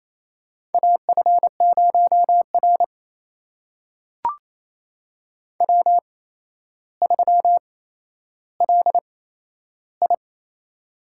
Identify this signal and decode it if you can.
Morse code, heard as a keyed tone, about 28 wpm, spaced as for 12 wpm, 700 Hz. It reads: AF0R W3LI
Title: USA Call Signs - 8x Character Spacing - 28wpm